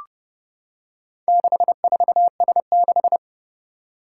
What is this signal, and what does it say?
Morse code, heard as a keyed tone, about 29 wpm, 700 Hz. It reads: 64S6
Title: Four Challenging Letter-Number Combinations 30wpm